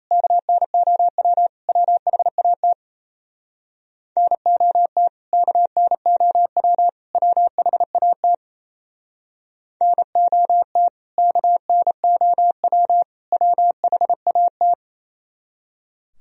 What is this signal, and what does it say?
Morse code, heard as a keyed tone, about 34 wpm, 700 Hz. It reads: KNOW WHAT NOT KNOW WHAT NOT KNOW WHAT
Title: Speed-Racing - Sets of 3 Words 28wpm